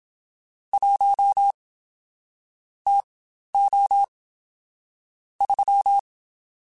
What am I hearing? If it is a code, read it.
Morse code, heard as a keyed tone, about 27 wpm, spaced as for 12 wpm, 800 Hz. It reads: 1 TO 3
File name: Spaced Characters / Spaced Characters-mp3s / Weather 5 1 sp @25